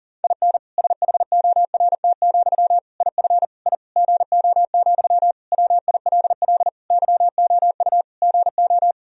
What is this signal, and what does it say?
Morse code, heard as a keyed tone, about 37 wpm, 700 Hz. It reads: IN SHORT, IF I GO, WILL YOU GO
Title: Sentences from Top 1000 Words - Review Part 2 - 40wpm